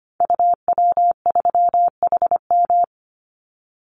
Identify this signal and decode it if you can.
Morse code, heard as a keyed tone, about 25 wpm, 700 Hz. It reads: UW3HM
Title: Call Signs - 25wpm